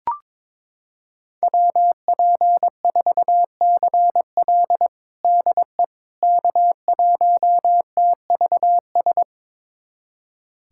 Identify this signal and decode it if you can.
Morse code, heard as a keyed tone, about 22 wpm, 700 Hz. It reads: WP4CL DE K1TVS
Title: USA Callsign DE Callsign - 5 Characters - 22wpm